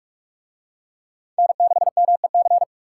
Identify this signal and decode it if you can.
Morse code, heard as a keyed tone, about 48 wpm, 700 Hz. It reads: N6MEC